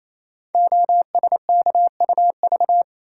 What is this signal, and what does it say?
Morse code, heard as a keyed tone, about 29 wpm, 700 Hz. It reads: OSKUV
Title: Code Groups - 5 Letters - 28wpm